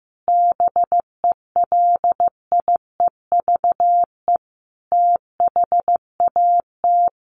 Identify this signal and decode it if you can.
Morse code, heard as a keyed tone, about 15 wpm, 700 Hz. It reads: BELIEVE THAT
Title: Sets of 5 Words Encore2 15wpm